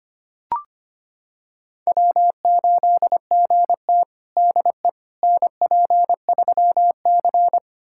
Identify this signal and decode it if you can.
Morse code, heard as a keyed tone, about 25 wpm, 700 Hz. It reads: W8GT DE NP3C
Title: USA Callsign DE Callsign - 4 Characters - 25wpm